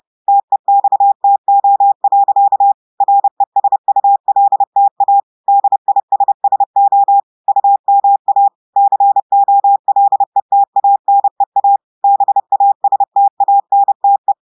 Morse code, heard as a keyed tone, about 30 wpm, 800 Hz. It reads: TEXTO. RESULTA DISSO UMA COLETANEA BASTANTE
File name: audios / portugais / portugais texte en clair 1 vitesse 30 mots minutes